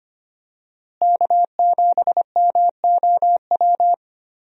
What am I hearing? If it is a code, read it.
Morse code, heard as a keyed tone, about 25 wpm, 700 Hz. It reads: K7MOW
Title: USA Call Signs - 5 Characters - Repeated 2x - 25wpm